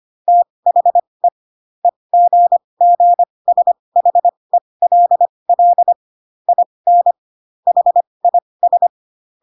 Morse code, heard as a keyed tone, about 25 wpm, spaced as for 20 wpm, 700 Hz. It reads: THE EGGSHELL IN HIS